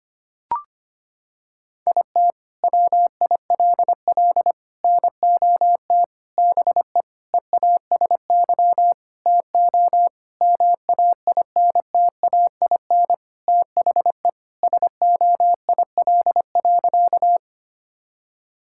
Morse code, heard as a keyed tone, about 25 wpm, 700 Hz. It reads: IT WILL NOT BE EASY TO MAINTAIN THE SOIL.
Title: Sentences from Top 1000 Words - Review Part 5 - 25wpm